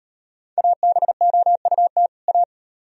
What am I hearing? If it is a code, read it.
Morse code, heard as a keyed tone, about 38 wpm, 700 Hz. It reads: ABOUT A